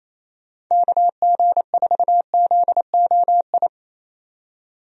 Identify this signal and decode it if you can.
Morse code, heard as a keyed tone, about 28 wpm, 700 Hz. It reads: KG4ZOI
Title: USA Call Signs - 6 Characters - 28wpm